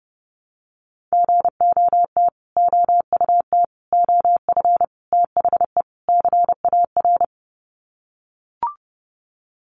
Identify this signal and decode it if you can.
Morse code, heard as a keyed tone, about 30 wpm, 700 Hz. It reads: GOT OUT OF THE CAR
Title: Sets of 5 Words Encore 30wpm